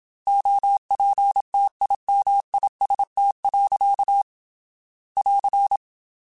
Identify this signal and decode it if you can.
Morse code, heard as a keyed tone, about 27 wpm, 800 Hz. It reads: OPTIMIST. <AR>
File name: Quotes / Quotes-mp3s / Optimist @25